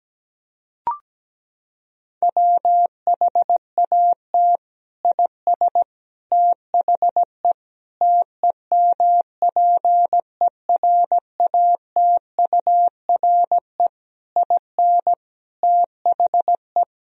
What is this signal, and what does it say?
Morse code, heard as a keyed tone, about 17 wpm, 700 Hz. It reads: WHAT IS THE TEMPERATURE IN THE
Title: Sentences from Top 1000 Words - Review Part 4 - 17wpm